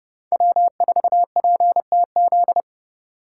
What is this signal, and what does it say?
Morse code, heard as a keyed tone, about 30 wpm, 700 Hz. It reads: W4PTZ